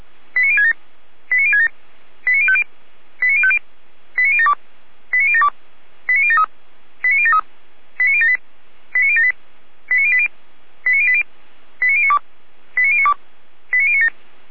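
Five-tone selective call (ZVEI), heard as quick runs of five tones, tones of about 70 ms. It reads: 89067 89067 89050 89050 89072 89072 89073 89073 89078 89078 89080 89080 89092 89092 89097
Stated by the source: ZVEI five-tone selective call received on SDR, from the Feuerwehr Orlamünde audio server.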